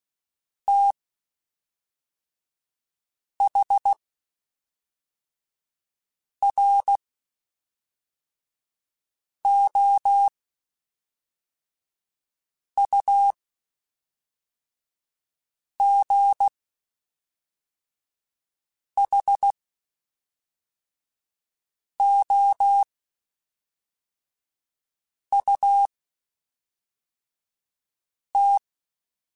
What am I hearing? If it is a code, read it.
Morse code, heard as a keyed tone, about 16 wpm, spaced as for 3 wpm, 800 Hz. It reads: THROUGHOUT
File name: Spaced Characters / Spaced Characters-mp3s / Weather5 12 sp @5